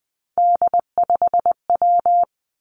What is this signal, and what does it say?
Morse code, heard as a keyed tone, about 20 wpm, 700 Hz. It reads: D5W
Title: Three Challenging Letter-Number Combinations 20wpm